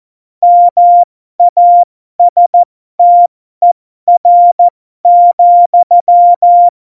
Morse code, heard as a keyed tone, about 14 wpm, spaced as for 12 wpm, 700 Hz. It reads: MASTER,
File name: Morsekurs / Witze